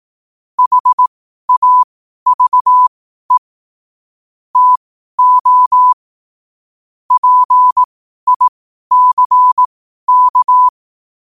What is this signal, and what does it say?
Morse code, heard as a keyed tone, about 18 wpm, spaced as for 12 wpm, 1000 Hz. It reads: HAVE TO PICK
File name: qotdfiles / 12wpm